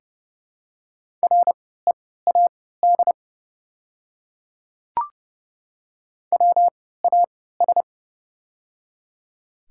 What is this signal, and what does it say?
Morse code, heard as a keyed tone, about 30 wpm, spaced as for 17 wpm, 700 Hz. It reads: READ WAS
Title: 3x Character Spacing - Top 500 Words - 30wpm